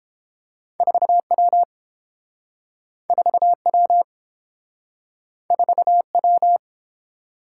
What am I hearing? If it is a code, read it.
Morse code, heard as a keyed tone, about 32 wpm, 700 Hz. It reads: 4W 4W 4W